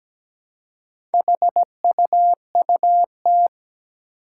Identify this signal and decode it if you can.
Morse code, heard as a keyed tone, about 17 wpm, 700 Hz. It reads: HUUT
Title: Four Challenging Letter-Number Combinations 17wpm